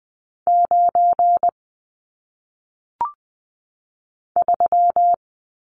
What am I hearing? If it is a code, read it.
Morse code, heard as a keyed tone, about 20 wpm, 700 Hz. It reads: 9 3
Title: Numbers - 1-Digit 20wpm